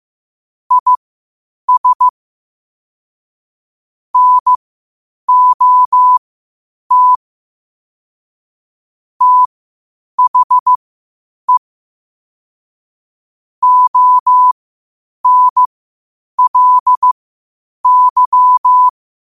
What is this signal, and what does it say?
Morse code, heard as a keyed tone, about 15 wpm, spaced as for 8 wpm, 1000 Hz. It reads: IS NOT THE ONLY
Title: 8 WPM morse code quotes for Thu, 29 Feb 2024